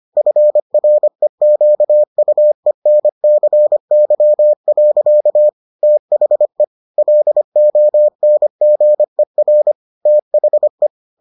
Morse code, heard as a keyed tone, about 25 wpm, 600 Hz. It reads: FREQUENCY. THE LONGER THE